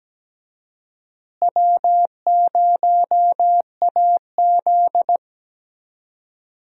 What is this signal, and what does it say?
Morse code, heard as a keyed tone, about 17 wpm, 700 Hz. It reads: W0AZ
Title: USA Call Signs - 4 Characters - Repeated 3x - 17wpm